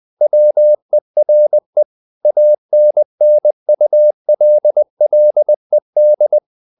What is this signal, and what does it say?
Morse code, heard as a keyed tone, about 20 wpm, 600 Hz. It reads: WERE ANNULLED
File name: the old man and the sea / the old man and the sea part 1 20wpm